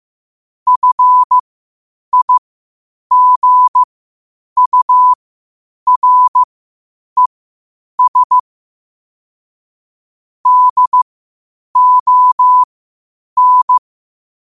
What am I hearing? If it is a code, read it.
Morse code, heard as a keyed tone, about 15 wpm, spaced as for 8 wpm, 1000 Hz. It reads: FIGURES DON